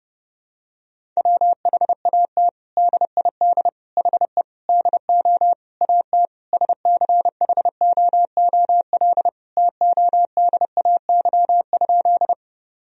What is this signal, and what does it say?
Morse code, heard as a keyed tone, about 30 wpm, 700 Hz. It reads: WHAT DID HE DO AT SCHOOL TODAY?
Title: Sentences from Top 1000 Words - Review Part 1 - 30wpm